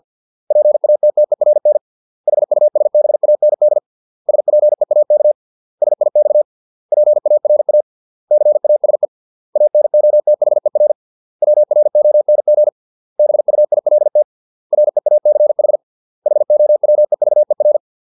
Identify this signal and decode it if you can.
Morse code, heard as a keyed tone, about 50 wpm, 600 Hz. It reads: PATTERN HUSBAND SPEAK SIX PARA CASE ANOTHER WRONG BUILT REACH HOWEVER
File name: 632 words / 632 Words 50WPM